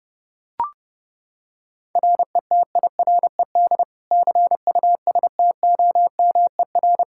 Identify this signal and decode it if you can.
Morse code, heard as a keyed tone, about 30 wpm, 700 Hz. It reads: RETIRED CUSTOMER
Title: Practice QSO - Exchange 3 - More Info - 30wpm